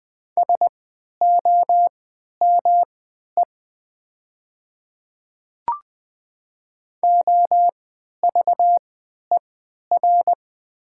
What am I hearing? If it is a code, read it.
Morse code, heard as a keyed tone, about 20 wpm, spaced as for 11 wpm, 700 Hz. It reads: SOME OVER
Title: 3x Character Spacing - Top 100 Words - 20wpm